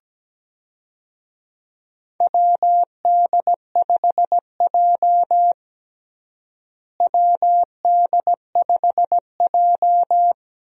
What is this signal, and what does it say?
Morse code, heard as a keyed tone, about 17 wpm, 700 Hz. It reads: WD5J WD5J
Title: USA Call Signs - 4 Characters - Repeated 2x - 17wpm